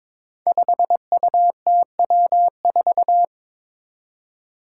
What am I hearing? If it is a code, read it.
Morse code, heard as a keyed tone, about 22 wpm, 700 Hz. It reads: 5UTW4
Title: Five Challenging Letter-Number Combinations 22wpm